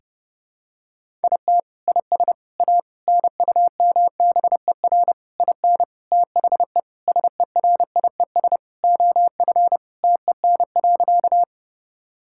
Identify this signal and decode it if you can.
Morse code, heard as a keyed tone, about 30 wpm, 700 Hz. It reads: IT IS A NUMBER IN THE SERIES OF TEN.
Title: Sentences from Top 1000 Words - Review Part 5 - 30wpm